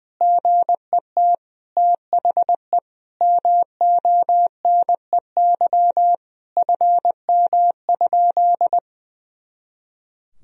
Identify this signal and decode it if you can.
Morse code, heard as a keyed tone, about 20 wpm, 700 Hz. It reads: GET THE MONEY FM?
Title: Sentences from Top 400 Words and Abbreviations 20wpm